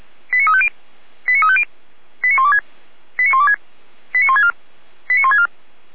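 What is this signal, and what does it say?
Five-tone selective call (ZVEI), heard as quick runs of five tones, tones of about 70 ms. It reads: 89350 89350 89126 89126 89164 89164
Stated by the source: Five-tone ZVEI selective call received on an SDR.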